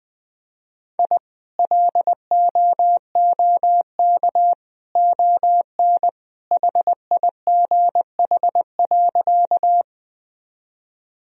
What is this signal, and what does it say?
Morse code, heard as a keyed tone, about 20 wpm, 700 Hz. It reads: I LOOK ON HIGH.
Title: Sentences from Top 1000 Words - Review Part 1 - 20wpm